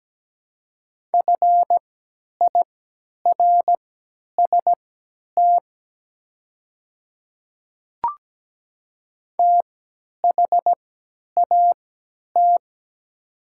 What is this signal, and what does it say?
Morse code, heard as a keyed tone, about 17 wpm, spaced as for 10 wpm, 700 Hz. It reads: FIRST THAT
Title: 3x Character Spacing - Top 200 Words - 17wpm